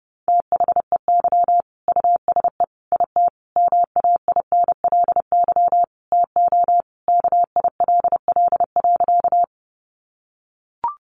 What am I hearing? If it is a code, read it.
Morse code, heard as a keyed tone, about 30 wpm, 700 Hz. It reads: THEY USE IT MAINLY TO KILL.
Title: Sentences from Top 1400 Words 30wpm